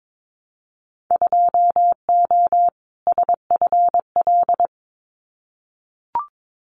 Morse code, heard as a keyed tone, about 22 wpm, 700 Hz. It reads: 2O SFL
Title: Winter Field Day Exchanges 22wpm